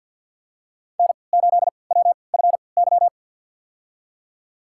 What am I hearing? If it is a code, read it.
Morse code, heard as a keyed tone, about 52 wpm, spaced as for 28 wpm, 700 Hz. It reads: N8WFY
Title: USA Call Signs - 5 Characters - 3x Character Spacing - 50wpm